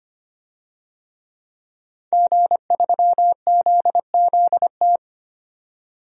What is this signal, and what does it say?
Morse code, heard as a keyed tone, about 25 wpm, 700 Hz. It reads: G3ZZT
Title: DXCC-100 Call Signs - 5 Characters - 25wpm